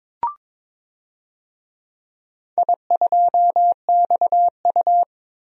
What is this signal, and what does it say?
Morse code, heard as a keyed tone, about 21 wpm, 700 Hz. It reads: I2XU
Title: DXCC-100 Call Signs - 4 Characters - 22wpm